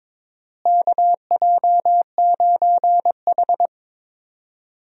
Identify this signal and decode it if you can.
Morse code, heard as a keyed tone, about 21 wpm, 700 Hz. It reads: KJ9H
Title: USA Call Signs - 4 Characters - Repeated 2x - 22wpm